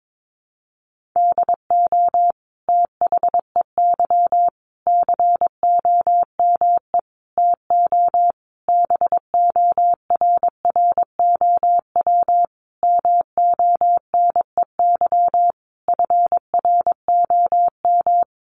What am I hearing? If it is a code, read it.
Morse code, heard as a keyed tone, about 22 wpm, 700 Hz. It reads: DO THEY COME TO BORROW MONEY FROM
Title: Sentences from Top 1500 Words 22wpm